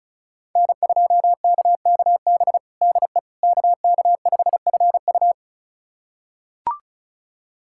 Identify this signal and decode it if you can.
Morse code, heard as a keyed tone, about 36 wpm, 700 Hz. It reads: N2KKB DE KK5FU
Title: USA Callsign DE Callsign 35wpm